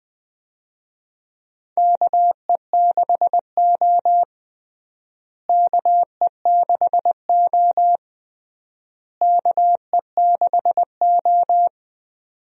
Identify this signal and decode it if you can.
Morse code, heard as a keyed tone, about 20 wpm, 700 Hz. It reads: KE6O KE6O KE6O
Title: USA Call Signs - 4 Characters - Repeated 3x - 20wpm